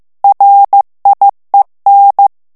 Morse code, heard as a keyed tone, about 15 wpm, 800 Hz. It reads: RIEN
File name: morse mp3 / FRANCAIS TEXTE CLAIR 2 VITESSE 14 MOTS MINUTES